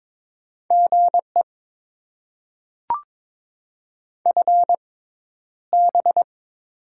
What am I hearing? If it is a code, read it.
Morse code, heard as a keyed tone, about 22 wpm, spaced as for 11 wpm, 700 Hz. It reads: GE FB
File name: Top 20 QSO Elements - 6x Character Spacing - 22wpm